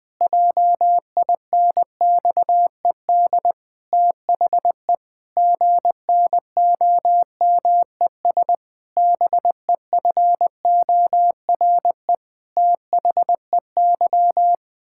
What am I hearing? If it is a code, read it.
Morse code, heard as a keyed tone, about 21 wpm, 700 Hz. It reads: JINXED THE GNOMES BEFORE THEY
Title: Pangram Challenge v2 - 20wpm